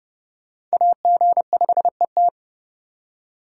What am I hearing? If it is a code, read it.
Morse code, heard as a keyed tone, about 37 wpm, 700 Hz. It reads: AG5ET
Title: Speed Racing - USA 2x2 Call Signs 30wpm